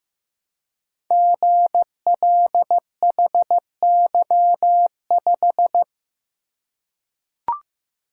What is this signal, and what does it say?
Morse code, heard as a keyed tone, about 15 wpm, 700 Hz. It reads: GLHY5
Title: Code Groups - 5 Letter:Numbers - 15wpm